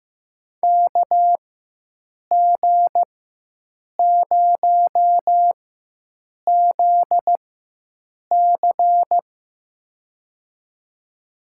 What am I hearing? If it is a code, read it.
Morse code, heard as a keyed tone, about 16 wpm, spaced as for 7 wpm, 700 Hz. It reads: KG0ZC
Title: USA Call Signs - 4x Character Spacing - 15wpm